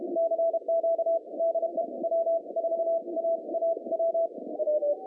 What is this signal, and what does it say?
Morse code, heard as a keyed tone, about 32 wpm, spaced as for 27 wpm, 660 Hz. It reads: CQ DE W4AAW
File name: 2015 cq ww cw / 2015 cqwww k5tr - 0-4